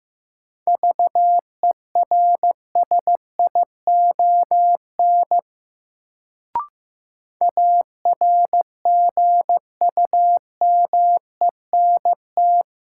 Morse code, heard as a keyed tone, about 15 wpm, 700 Hz. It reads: VERSION ARGUMENT
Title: Sentences from Top 1400 Words - New Word Drill - 15wpm